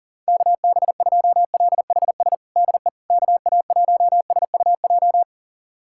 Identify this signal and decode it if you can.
Morse code, heard as a keyed tone, about 40 wpm, 700 Hz. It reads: KB2LHS DE KA1SUJ
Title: USA Callsign DE Callsign 40wpm